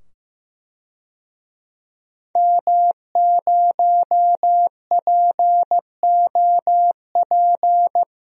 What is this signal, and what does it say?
Morse code, heard as a keyed tone, about 15 wpm, 700 Hz. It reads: M0POP
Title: DXCC-100 Call Signs - 5 Characters - 15wpm